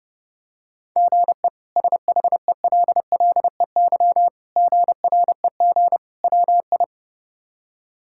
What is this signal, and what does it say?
Morse code, heard as a keyed tone, about 32 wpm, 700 Hz. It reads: GE SHELLEY GREG WI
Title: K1USN SST QSO Practice - 30wpm